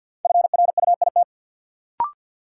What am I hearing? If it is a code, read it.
Morse code, heard as a keyed tone, about 52 wpm, 700 Hz. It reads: FRUIT
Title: Sentences from Top 1300 Words - New Word Drill - 50wpm